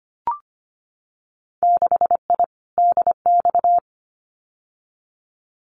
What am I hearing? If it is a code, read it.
Morse code, heard as a keyed tone, about 25 wpm, 690 Hz. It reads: 6I DX
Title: Winter Field Day Exchanges Encore 25wpm